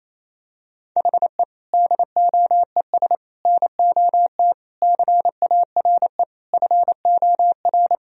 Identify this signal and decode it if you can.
Morse code, heard as a keyed tone, about 28 wpm, 700 Hz. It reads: HE DOES NOT CARE FOR
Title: Sentences from Top 1000 Words - Review Part 1 - 28wpm